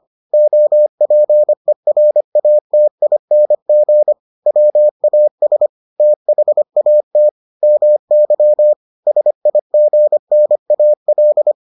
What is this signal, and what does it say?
Morse code, heard as a keyed tone, about 25 wpm, 600 Hz. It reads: OPERATING WAS THAT MY SIGNAL